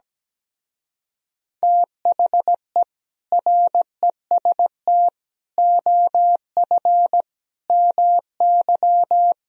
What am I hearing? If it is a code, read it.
Morse code, heard as a keyed tone, about 17 wpm, 700 Hz. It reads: THE REST OF MY
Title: Speed-Racing - Sets of 5 Words 17wpm